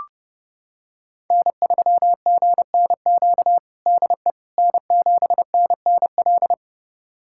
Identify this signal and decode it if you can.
Morse code, heard as a keyed tone, about 30 wpm, 700 Hz. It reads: N3GNQ DE N7NNL
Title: USA Callsign DE Callsign - 5 Characters - 30wpm